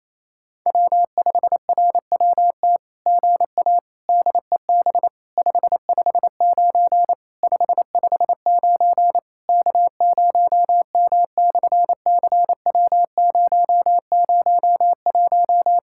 Morse code, heard as a keyed tone, about 27 wpm, 700 Hz. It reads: W5RWT GA DEB 559 559 K0M/CW001
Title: SOTA QSO Practice 28wpm